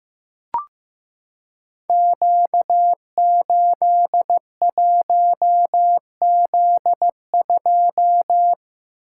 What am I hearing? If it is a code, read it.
Morse code, heard as a keyed tone, about 15 wpm, 700 Hz. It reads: Q81Z2
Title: Code Groups - 5 Letter:Numbers - 15wpm